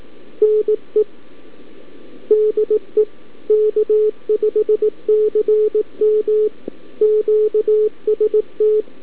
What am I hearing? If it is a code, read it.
Morse code, heard as a keyed tone, about 18 wpm, 430 Hz. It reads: NE DE K5CM QST
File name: k5cm-20mNE Callup-by-abi